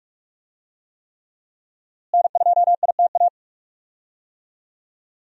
Morse code, heard as a keyed tone, about 46 wpm, 700 Hz. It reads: N2ITA
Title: USA Call Signs - 5 Characters - Repeated 3x - 45wpm